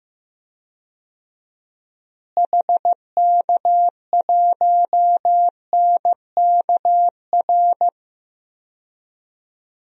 Morse code, heard as a keyed tone, about 15 wpm, 700 Hz. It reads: HK1NKR
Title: DXCC-100 Call Signs - 6 Characters - Repeated 2x - 15wpm